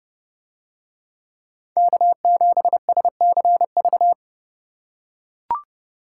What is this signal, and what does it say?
Morse code, heard as a keyed tone, about 30 wpm, 700 Hz. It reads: K7SCV